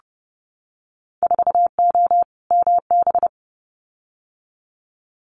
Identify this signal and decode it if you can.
Morse code, heard as a keyed tone, about 30 wpm, 700 Hz. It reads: 4O MB
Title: Winter Field Day Exchanges Encore 30wpm